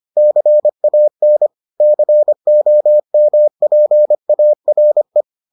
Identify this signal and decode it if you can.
Morse code, heard as a keyed tone, about 25 wpm, 600 Hz. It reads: CAN COMPARE